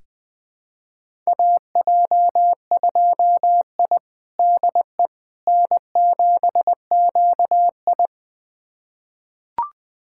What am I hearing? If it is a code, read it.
Morse code, heard as a keyed tone, about 20 wpm, 700 Hz. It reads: AJ2I DE N7QI
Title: USA Callsign DE Callsign - 4 Characters - 20wpm